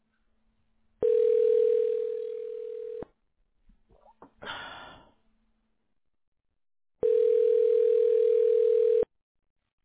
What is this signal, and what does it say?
Ringback tone (North America), 440 + 480 Hz.